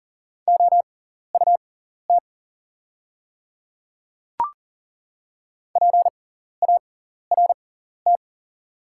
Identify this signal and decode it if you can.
Morse code, heard as a keyed tone, about 40 wpm, spaced as for 14 wpm, 690 Hz. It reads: OUT PART
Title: 6x Character Spacing - Top 200 Words - 40wpm